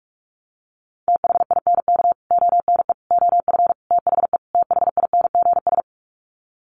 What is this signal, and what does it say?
Morse code, heard as a keyed tone, about 44 wpm, 700 Hz. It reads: THINK ONE OF THE THINGS
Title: Sets of 5 Words Encore2 45wpm